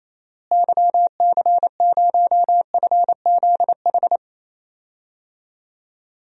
Morse code, heard as a keyed tone, about 28 wpm, 700 Hz. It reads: YC0FZH
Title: DXCC-100 Call Signs - 6 Characters - Repeated 2x - 28wpm